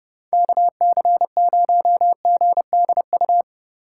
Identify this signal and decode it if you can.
Morse code, heard as a keyed tone, about 32 wpm, 700 Hz. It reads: KC0GDU